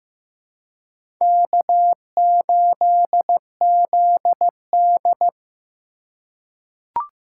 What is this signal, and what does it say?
Morse code, heard as a keyed tone, about 15 wpm, 700 Hz. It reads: K8ZD
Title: USA Call Signs - 4 Characters - 4x Character Spacing - 15wpm